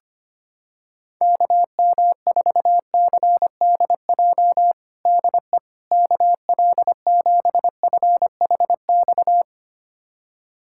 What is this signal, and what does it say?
Morse code, heard as a keyed tone, about 25 wpm, 700 Hz. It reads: KM4CDJ DE KL7FHX
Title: USA Callsign DE Callsign - 6 Characters - 25wpm